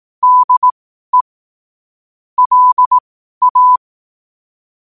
Morse code, heard as a keyed tone, about 18 wpm, spaced as for 12 wpm, 1000 Hz. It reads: DE LA
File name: qotdfiles / 12wpm